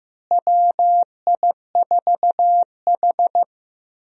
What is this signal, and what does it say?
Morse code, heard as a keyed tone, about 15 wpm, 700 Hz. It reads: WI4H
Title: USA Call Signs - 4 Characters - Repeated 2x - 15wpm